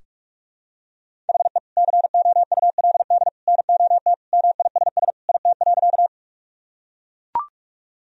Morse code, heard as a keyed tone, about 41 wpm, 700 Hz. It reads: HE COULD NOT MISS IT.
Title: Sentences from Top 1000 Words - Review Part 3 - 45wpm